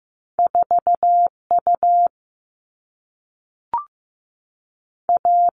Morse code, heard as a keyed tone, about 15 wpm, 700 Hz. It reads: 4U A
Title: All Cut Numbers - 2-Digit 15wpm